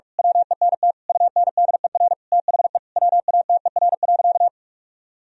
Morse code, heard as a keyed tone, about 41 wpm, 700 Hz. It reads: WENT UNDER THE WATER.
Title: Sentences from Top 1000 Words - Review Part 1 - 45wpm